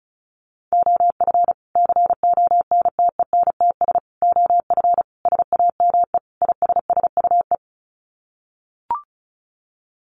Morse code, heard as a keyed tone, about 34 wpm, 700 Hz. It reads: OF CONTENTS OF SAME ISSUE
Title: Sets of 5 Words Encore2 35wpm